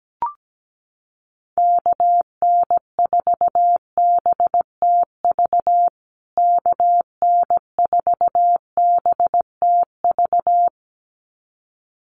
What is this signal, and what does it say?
Morse code, heard as a keyed tone, about 17 wpm, 700 Hz. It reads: KN4BTV KN4BTV
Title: USA Call Signs Repeated 2x 17wpm